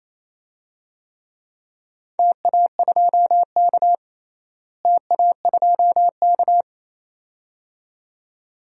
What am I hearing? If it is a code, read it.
Morse code, heard as a keyed tone, about 28 wpm, 700 Hz. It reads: TA2K TA2K
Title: DXCC-100 Call Signs - 4 Characters - Repeated 2x - 28wpm